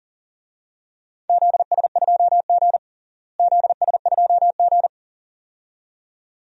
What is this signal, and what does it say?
Morse code, heard as a keyed tone, about 40 wpm, 700 Hz. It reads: ZS2G ZS2G
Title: DXCC-100 Call Signs - 4 Characters - Repeated 2x - 40wpm